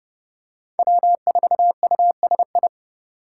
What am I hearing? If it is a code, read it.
Morse code, heard as a keyed tone, about 29 wpm, 700 Hz. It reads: W4USI
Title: K1USN SST QSO Practice - 30wpm